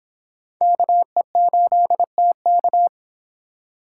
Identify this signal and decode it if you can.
Morse code, heard as a keyed tone, about 26 wpm, 700 Hz. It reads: KE8TK